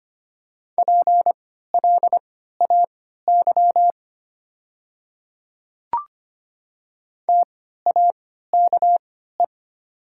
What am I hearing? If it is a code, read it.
Morse code, heard as a keyed tone, about 25 wpm, spaced as for 14 wpm, 690 Hz. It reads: PLAY TAKE
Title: 3x Character Spacing - Top 300 Words - 25wpm